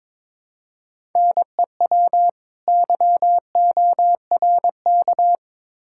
Morse code, heard as a keyed tone, about 22 wpm, 700 Hz. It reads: NEW YORK
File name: Build A Phrase - 2-Word Phrases - 22wpm